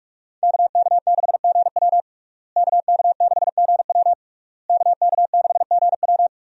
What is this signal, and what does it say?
Morse code, heard as a keyed tone, about 46 wpm, 700 Hz. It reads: KK6GW KK6GW KK6GW
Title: USA Call Signs - 5 Characters - Repeated 3x - 45wpm